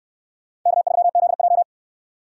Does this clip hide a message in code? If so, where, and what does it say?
Yes: Morse code from 0.7 to 1.6 s, heard as a keyed tone, about 62 wpm, 700 Hz. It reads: K3ZJ